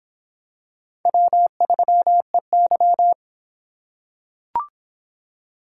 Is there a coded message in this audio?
Yes: Morse code from 1.1 to 3.1 s, heard as a keyed tone, about 24 wpm, 700 Hz. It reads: W3EY